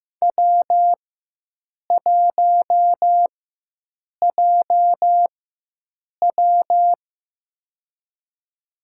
Morse code, heard as a keyed tone, about 15 wpm, spaced as for 7 wpm, 700 Hz. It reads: W1JW
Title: USA Call Signs - 4 Characters - 4x Character Spacing - 15wpm